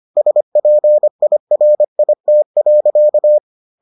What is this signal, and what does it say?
Morse code, heard as a keyed tone, about 25 wpm, 600 Hz. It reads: SPIRIT.